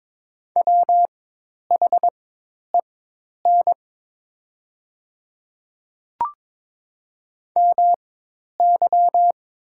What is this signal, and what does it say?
Morse code, heard as a keyed tone, about 22 wpm, spaced as for 10 wpm, 700 Hz. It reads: WHEN MY
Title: 4x Character Spacing - Top 200 Words - 22wpm